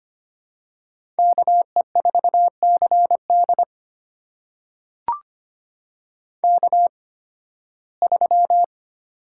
Morse code, heard as a keyed tone, about 25 wpm, spaced as for 16 wpm, 700 Hz. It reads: KE4CD K3
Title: USA Call Signs - 8x Character Spacing - 25wpm